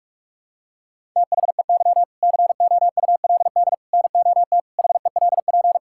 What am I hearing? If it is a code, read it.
Morse code, heard as a keyed tone, about 46 wpm, 700 Hz. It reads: THEY COULD NOT HELP
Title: Sentences from Top 1000 Words - Review Part 5 - 45wpm